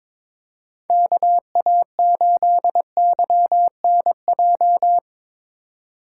Morse code, heard as a keyed tone, about 22 wpm, 700 Hz. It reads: KA8YNJ
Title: USA Call Signs - 3x Character Spacing - 22wpm